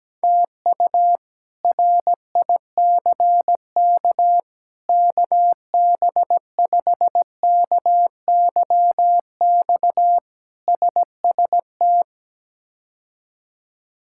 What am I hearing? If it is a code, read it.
Morse code, heard as a keyed tone, about 17 wpm, 700 Hz. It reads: TU RICK KB5KYX SST
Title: K1USN SST QSO Practice - 17wpm